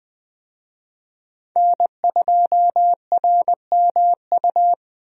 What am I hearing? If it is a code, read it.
Morse code, heard as a keyed tone, about 20 wpm, 700 Hz. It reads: N2RMU
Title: USA Call Signs - 5 Characters - Repeated 2x - 20wpm